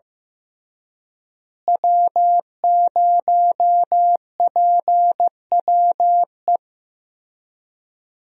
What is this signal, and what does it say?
Morse code, heard as a keyed tone, about 15 wpm, 700 Hz. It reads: W0PWE
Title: USA Call Signs - 5 Characters - Repeated 3x - 15wpm